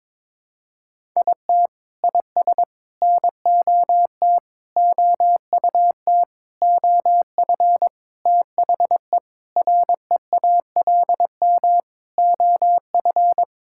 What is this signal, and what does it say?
Morse code, heard as a keyed tone, about 22 wpm, 700 Hz. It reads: IT IS NOT OUT OF THE REALM OF
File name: Sentences with 9-Word Phrases - 22wpm